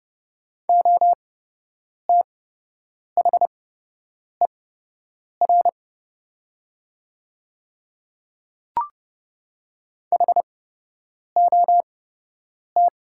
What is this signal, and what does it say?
Morse code, heard as a keyed tone, about 31 wpm, spaced as for 8 wpm, 700 Hz. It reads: OTHER HOT
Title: 8x Character Spacing - Top 100 Words - 30wpm